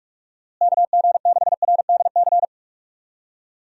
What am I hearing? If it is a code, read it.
Morse code, heard as a keyed tone, about 42 wpm, 700 Hz. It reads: KG6RDC
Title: USA Call Signs - 6 Characters - Repeated 2x - 45wpm